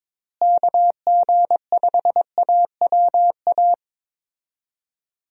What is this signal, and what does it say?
Morse code, heard as a keyed tone, about 22 wpm, 700 Hz. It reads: KG5AWA
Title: USA Call Signs - 6 Characters - Repeated 2x - 22wpm